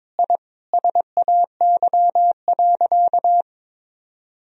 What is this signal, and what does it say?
Morse code, heard as a keyed tone, about 22 wpm, 700 Hz. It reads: I SAY.
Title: Sentences from Top 1000 Words - Review Part 2 - 22wpm